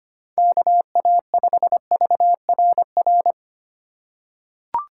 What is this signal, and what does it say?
Morse code, heard as a keyed tone, about 25 wpm, 700 Hz. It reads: KA5VRR
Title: USA Call Signs - 6 Characters - 3x Character Spacing - 25wpm